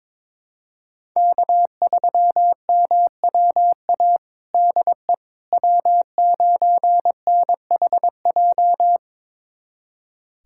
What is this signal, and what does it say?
Morse code, heard as a keyed tone, about 22 wpm, 700 Hz. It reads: K3MWA DE W9NHJ
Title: USA Callsign DE Callsign - 5 Characters - 22wpm